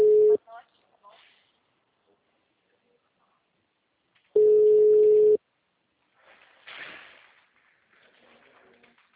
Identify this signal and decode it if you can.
Ringback tone (Europe, 425 Hz), with speech.
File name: out- -2047- - - .1582